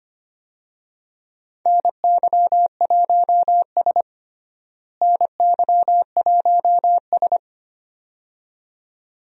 Morse code, heard as a keyed tone, about 25 wpm, 700 Hz. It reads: NY1S NY1S
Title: USA Call Signs - 4 Characters - Repeated 2x - 25wpm